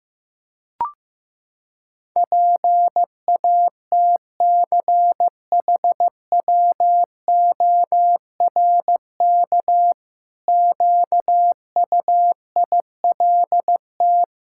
Morse code, heard as a keyed tone, about 15 wpm, 700 Hz. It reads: PATCHWORK QUILT
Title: Pangram Challenge v2 - 15wpm